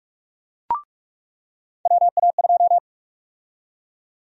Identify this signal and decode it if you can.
Morse code, heard as a keyed tone, about 46 wpm, 700 Hz. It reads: WA2
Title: USA Call Sign Prefixes - 45wpm